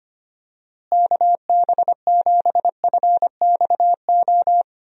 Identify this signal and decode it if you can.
Morse code, heard as a keyed tone, about 25 wpm, 700 Hz. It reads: KB7FXO
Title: USA Call Signs - 6 Characters - Repeated 2x - 25wpm